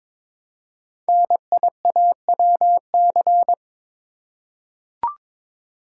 Morse code, heard as a keyed tone, about 22 wpm, 700 Hz. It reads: NIAWC
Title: Code Groups - 5 Letters - 22wpm